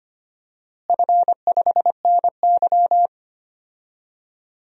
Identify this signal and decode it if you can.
Morse code, heard as a keyed tone, about 25 wpm, 700 Hz. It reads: F5NY
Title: DXCC-100 Call Signs - 4 Characters - Repeated 2x - 25wpm